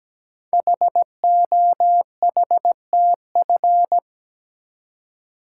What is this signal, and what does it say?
Morse code, heard as a keyed tone, about 17 wpm, 700 Hz. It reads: HOHTF
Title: Code Groups - 5 Letters - 17wpm